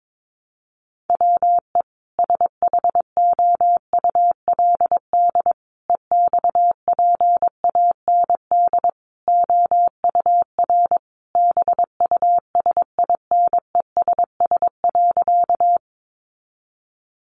Morse code, heard as a keyed tone, about 22 wpm, 700 Hz. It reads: WE SHOULD EXPAND OUR BUSINESS.
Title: Sentences from Top 1400 Words 22wpm